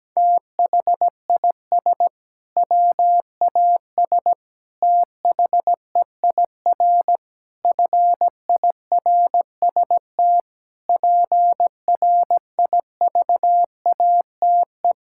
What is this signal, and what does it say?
Morse code, heard as a keyed tone, about 17 wpm, 700 Hz. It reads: THIS WAS THEIR FIRST PRIVATE
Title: Sentences from Top 1600 Words 17wpm